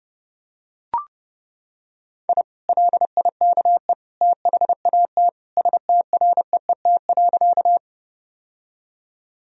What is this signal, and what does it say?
Morse code, heard as a keyed tone, about 32 wpm, 700 Hz. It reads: I LIKE THAT STREET.
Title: Sentences from Top 1000 Words - Review Part 3 - 30wpm